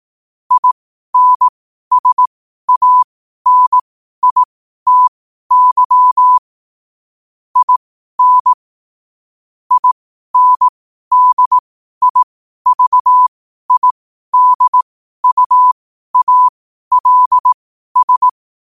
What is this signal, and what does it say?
Morse code, heard as a keyed tone, about 18 wpm, spaced as for 12 wpm, 1000 Hz. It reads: INSANITY IN INDIVIDUALS
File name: qotdfiles / 12wpm